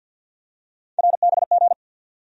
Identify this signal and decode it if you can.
Morse code, heard as a keyed tone, about 50 wpm, 700 Hz. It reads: RBG
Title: USA Call Sign Postfixes - 3 Letters 50wpm